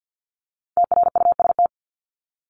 Morse code, heard as a keyed tone, about 50 wpm, 700 Hz. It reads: TRUST
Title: Top 1751-2000 Words 50wpm